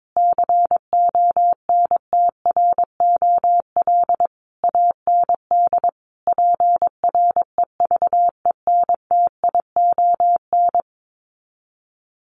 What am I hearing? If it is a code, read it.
Morse code, heard as a keyed tone, about 22 wpm, 700 Hz. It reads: CONTROL AND PREVENTION
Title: Sets of 5 Words Encore2 22wpm